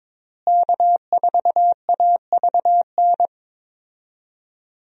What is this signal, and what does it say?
Morse code, heard as a keyed tone, about 22 wpm, 700 Hz. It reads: K4AVN
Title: USA Call Signs - 5 Characters - Repeated 3x - 22wpm